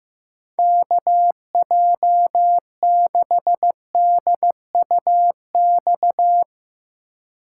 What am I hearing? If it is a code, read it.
Morse code, heard as a keyed tone, about 15 wpm, 700 Hz. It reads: KJ6DUX
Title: USA Call Signs - 6 Characters - 4x Character Spacing - 15wpm